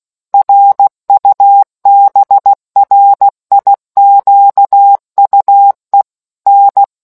Morse code, heard as a keyed tone, about 16 wpm, 800 Hz. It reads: RUBRIQUE N